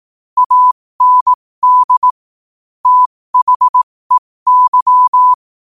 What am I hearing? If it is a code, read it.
Morse code, heard as a keyed tone, about 18 wpm, spaced as for 15 wpm, 1000 Hz. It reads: AND THEY
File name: qotdfiles / 15wpm